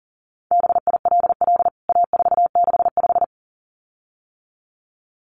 Morse code, heard as a keyed tone, about 40 wpm, 700 Hz. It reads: BILL A465
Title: CWT Exchange 40wpm